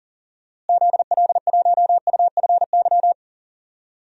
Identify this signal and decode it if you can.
Morse code, heard as a keyed tone, about 38 wpm, 700 Hz. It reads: ZL1UFY
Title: DXCC-100 Call Signs - 6 Characters - Repeated 2x - 40wpm